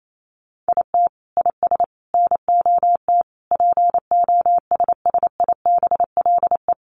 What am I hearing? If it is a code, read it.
Morse code, heard as a keyed tone, about 28 wpm, 700 Hz. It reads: IT IS NOT POSSIBLE
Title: Sets of 5 Words Encore2 28wpm